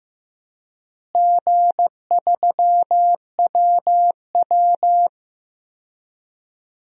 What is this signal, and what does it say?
Morse code, heard as a keyed tone, about 15 wpm, 700 Hz. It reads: G3WW
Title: DXCC-100 Call Signs - 4 Characters - Repeated 3x - 15wpm